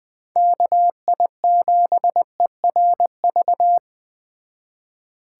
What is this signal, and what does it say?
Morse code, heard as a keyed tone, about 20 wpm, 700 Hz. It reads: KI7ERV
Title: Speed Racing - USA 2x3 Call Signs 17wpm